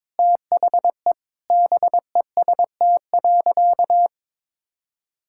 Morse code, heard as a keyed tone, about 22 wpm, 700 Hz. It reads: THE BEST.
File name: practice / 22wpm / Sentences with 3-Word Phrases - 22wpm